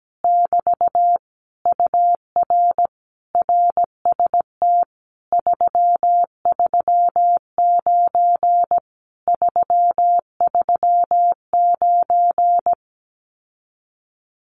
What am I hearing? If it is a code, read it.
Morse code, heard as a keyed tone, about 17 wpm, 700 Hz. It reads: <BT> UR RST 339 339
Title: Common QSO Phrases Encore 17wpm